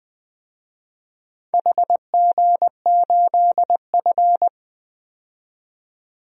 Morse code, heard as a keyed tone, about 20 wpm, 700 Hz. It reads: HG8F
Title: DXCC-100 Call Signs - 4 Characters - 20wpm